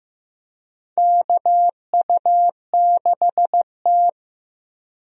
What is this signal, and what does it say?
Morse code, heard as a keyed tone, about 16 wpm, 700 Hz. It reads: KU6T
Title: USA Call Signs - 4 Characters - Repeated 2x - 15wpm